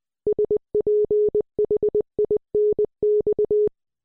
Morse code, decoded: SPHINX